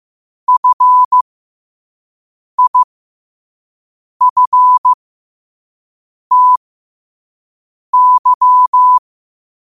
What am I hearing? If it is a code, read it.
Morse code, heard as a keyed tone, about 15 wpm, spaced as for 5 wpm, 1000 Hz. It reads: FIFTY
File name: qotdfiles / 5wpm